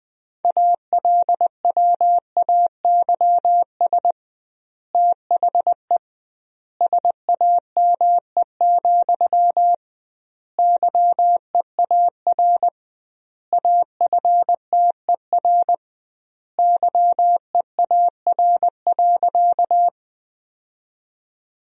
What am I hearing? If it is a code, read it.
Morse code, heard as a keyed tone, about 20 wpm, 700 Hz. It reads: ALWAYS THE SAME, YEAR AFTER YEAR.
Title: 2x Word-Spacing - Sentences from Top 300 Words - 20wpm